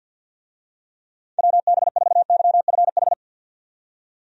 Morse code, heard as a keyed tone, about 50 wpm, 700 Hz. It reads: WB4YFH
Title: USA Call Signs - 6 Characters - 50wpm